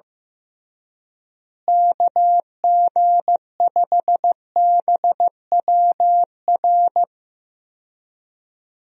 Morse code, heard as a keyed tone, about 16 wpm, 700 Hz. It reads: KG5BWR
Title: USA Call Signs - 6 Characters - Repeated 3x - 15wpm